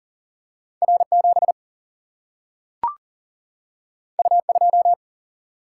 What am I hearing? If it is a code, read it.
Morse code, heard as a keyed tone, about 39 wpm, 700 Hz. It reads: R 7 U 2
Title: Single-Letter Single-Number 40wpm